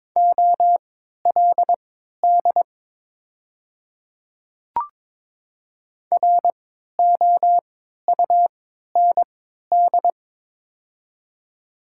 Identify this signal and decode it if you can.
Morse code, heard as a keyed tone, about 22 wpm, spaced as for 13 wpm, 690 Hz. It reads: OLD ROUND
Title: 3x Character Spacing - Top 200 Words - 22wpm